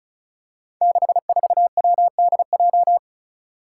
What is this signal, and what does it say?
Morse code, heard as a keyed tone, about 34 wpm, 700 Hz. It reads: 64WDJ
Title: Five Challenging Letter-Number Combinations 35wpm